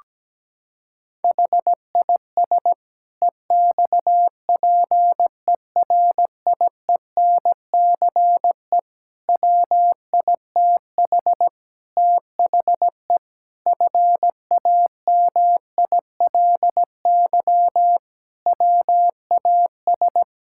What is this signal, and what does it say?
Morse code, heard as a keyed tone, about 17 wpm, 700 Hz. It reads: HIS EXPERIENCE WITH THE FAMILY WAS
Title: Sentences from Top 1000 Words - Review Part 6 - 17wpm